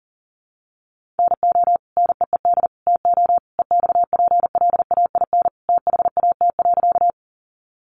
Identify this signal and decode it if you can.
Morse code, heard as a keyed tone, about 40 wpm, 700 Hz. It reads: NO NEED TO EXPLAIN THAT.
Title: Sentences from Top 1100 Words 40wpm